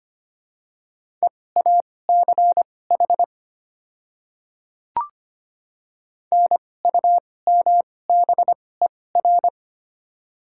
Morse code, heard as a keyed tone, about 25 wpm, spaced as for 18 wpm, 690 Hz. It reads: EACH NUMBER
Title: 2x Character Spacing - Top 200 Words - 25wpm